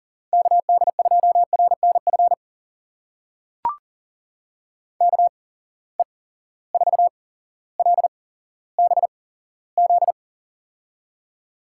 Morse code, heard as a keyed tone, about 38 wpm, spaced as for 17 wpm, 700 Hz. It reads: KD2RNF KE4LBZ
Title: USA Call Signs - 6 Characters - 8x Character Spacing - 40wpm